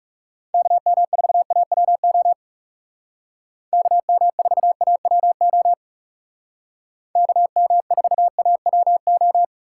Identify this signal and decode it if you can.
Morse code, heard as a keyed tone, about 35 wpm, 700 Hz. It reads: KM4AWO KM4AWO KM4AWO
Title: Speed Racing - USA 2x3 Call Signs 30wpm